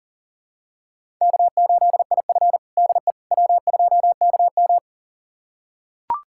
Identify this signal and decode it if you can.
Morse code, heard as a keyed tone, about 40 wpm, 700 Hz. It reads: K8IF DE W2KM
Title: USA Callsign DE Callsign - 4 Characters - 40wpm